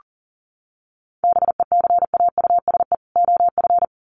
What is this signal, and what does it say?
Morse code, heard as a keyed tone, about 37 wpm, 700 Hz. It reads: BECAUSE OF
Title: Sets of 2 Words 40wpm